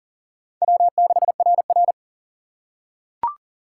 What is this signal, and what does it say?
Morse code, heard as a keyed tone, about 40 wpm, 700 Hz. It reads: W6RR
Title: USA Call Signs - 4 Characters - 4x Character Spacing - 40wpm